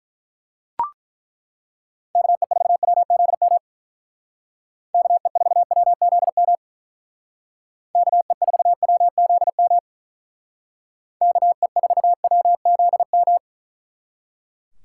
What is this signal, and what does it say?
Morse code, heard as a keyed tone, about 41 wpm, 700 Hz. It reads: KE4WZM KE4WZM KE4WZM KE4WZM
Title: Speed Racing - USA Call Signs 35wpm